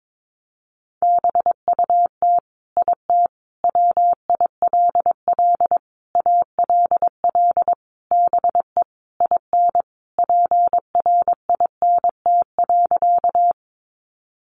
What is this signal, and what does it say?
Morse code, heard as a keyed tone, about 22 wpm, 700 Hz. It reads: BUT IT WILL ALL BE IN PRINT.
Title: Sentences from Top 1400 Words 22wpm